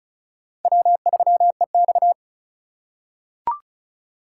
Morse code, heard as a keyed tone, about 34 wpm, 700 Hz. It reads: W3EX
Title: USA Call Signs - 4 Characters - 4x Character Spacing - 35wpm